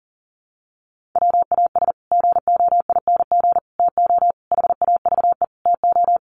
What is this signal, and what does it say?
Morse code, heard as a keyed tone, about 40 wpm, 700 Hz. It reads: WAS GOING TO HAVE TO
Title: Sets of 5 Words Encore 40wpm